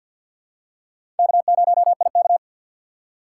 Morse code, heard as a keyed tone, about 50 wpm, 700 Hz. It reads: K0IK
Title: USA Call Signs - 3x Character Spacing - 50wpm